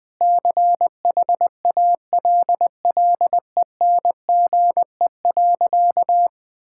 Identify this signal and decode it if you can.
Morse code, heard as a keyed tone, about 21 wpm, 700 Hz. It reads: CHALLENGE.
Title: Sentences with 4-Word Phrases - 20wpm